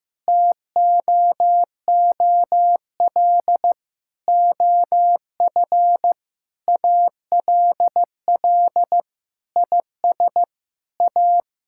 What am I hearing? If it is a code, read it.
Morse code, heard as a keyed tone, about 15 wpm, 700 Hz. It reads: TOOL OF ALL IS A
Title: Sentences from Top 1000 Words - Review Part 6 - 15wpm